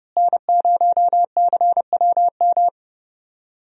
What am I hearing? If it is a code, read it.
Morse code, heard as a keyed tone, about 32 wpm, 700 Hz. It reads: N0CWM
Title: USA Call Signs - 5 Characters - Repeated 3x - 30wpm